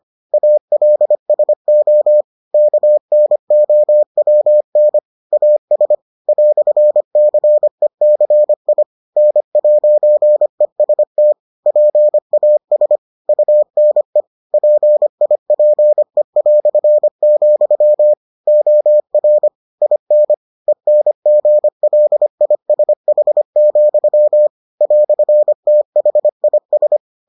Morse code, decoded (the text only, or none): ALSO KNOWN AS "CECI N'EST PAS UNE PIPE", OR IN ENGLISH, "THIS